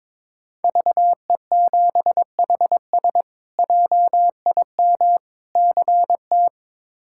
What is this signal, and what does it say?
Morse code, heard as a keyed tone, about 22 wpm, 700 Hz. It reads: VE7HS JIM CT
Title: K1USN SST QSO Practice - 22wpm